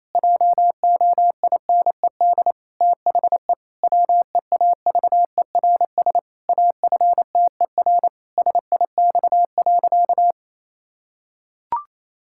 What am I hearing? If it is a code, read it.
Morse code, heard as a keyed tone, about 28 wpm, 700 Hz. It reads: JOINED THE WEAVERS AFTER SIX.
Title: Pangram Challenge v2 - 28wpm